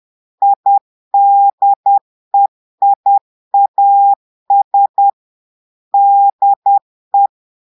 Morse code, heard as a keyed tone, about 10 wpm, 800 Hz. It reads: IDEIAS DE
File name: portugais texte en clair 2 vitesse 10 mots minutes